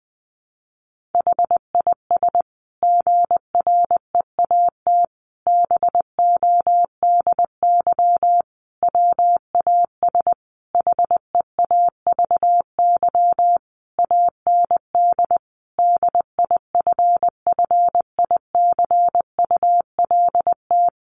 Morse code, heard as a keyed tone, about 20 wpm, 700 Hz. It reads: HIS GREAT BODY WAS HEAVY AND DIFFICULT
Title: Sentences from Top 1400 Words 20wpm